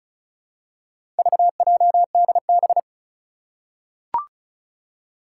Morse code, heard as a keyed tone, about 36 wpm, 700 Hz. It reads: VJDB
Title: Four Challenging Letter-Number Combinations 35wpm